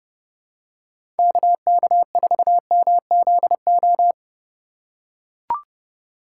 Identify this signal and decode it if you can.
Morse code, heard as a keyed tone, about 30 wpm, 700 Hz. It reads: KK4MZO
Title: USA Call Signs - 6 Characters - 3x Character Spacing - 30wpm